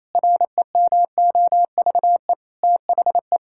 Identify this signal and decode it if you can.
Morse code, heard as a keyed tone, about 28 wpm, 700 Hz. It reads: REMOVE THE
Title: Sentences from Top 1000 Words - Review Part 5 - 28wpm